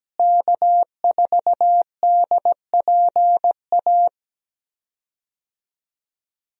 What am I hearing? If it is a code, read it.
Morse code, heard as a keyed tone, about 17 wpm, 700 Hz. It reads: K4DPA
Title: USA Call Signs - 5 Characters - Repeated 2x - 17wpm